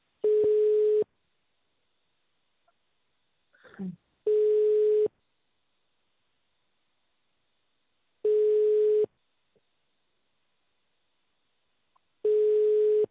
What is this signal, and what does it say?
Ringback tone (425 Hz, 1 s on, 3 s off).